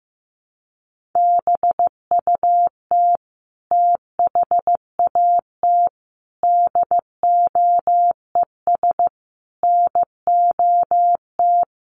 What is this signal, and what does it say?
Morse code, heard as a keyed tone, about 15 wpm, 700 Hz. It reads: BUT THAT DOES NOT
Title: Sets of 5 Words 15wpm